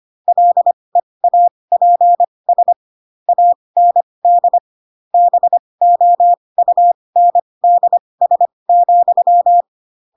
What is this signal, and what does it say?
Morse code, heard as a keyed tone, about 25 wpm, spaced as for 20 wpm, 700 Hz. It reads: LEAPS AND BOUNDS,
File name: Pinocchio / Pinocchio ch6 15 25 ik1hge